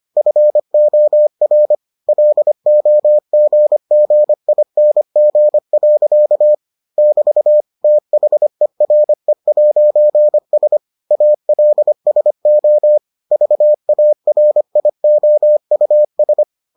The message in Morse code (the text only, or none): FOR LOGGING. <BT> THERE'S ALSO VARIOUS